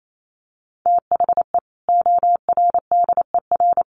Morse code, heard as a keyed tone, about 28 wpm, 700 Hz. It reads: THE ORDER
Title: Sentences from Top 1200 Words 28wpm